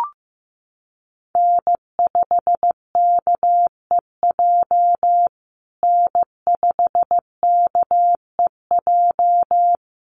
Morse code, heard as a keyed tone, about 15 wpm, 700 Hz. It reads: N5KEJ N5KEJ
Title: USA Call Signs Repeated 2x 15wpm